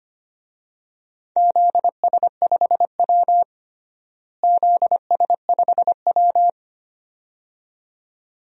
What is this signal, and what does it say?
Morse code, heard as a keyed tone, about 25 wpm, 700 Hz. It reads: ZS5W ZS5W
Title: DXCC-100 Call Signs - 4 Characters - Repeated 2x - 25wpm